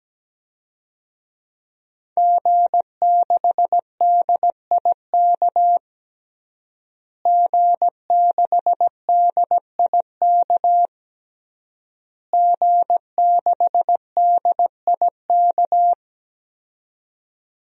Morse code, heard as a keyed tone, about 17 wpm, 700 Hz. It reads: G6DIK G6DIK G6DIK
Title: DXCC-100 Call Signs - 5 Characters - Repeated 3x - 17wpm